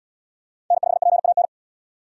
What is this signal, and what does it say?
Morse code, heard as a keyed tone, about 67 wpm, 700 Hz. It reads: N5PAN